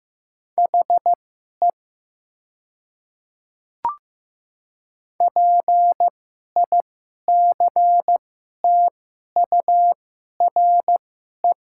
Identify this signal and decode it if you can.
Morse code, heard as a keyed tone, about 15 wpm, spaced as for 11 wpm, 700 Hz. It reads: HE PICTURE
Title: 2x Character Spacing - Top 200 Words - 15wpm